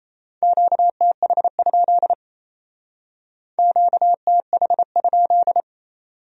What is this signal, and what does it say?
Morse code, heard as a keyed tone, about 35 wpm, 700 Hz. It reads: QTH? QTH?